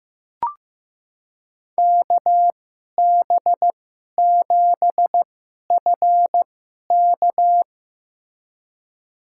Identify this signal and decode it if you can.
Morse code, heard as a keyed tone, about 15 wpm, spaced as for 11 wpm, 700 Hz. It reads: KB7FK
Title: USA Call Signs - 5 Characters - 2x Character Spacing - 15wpm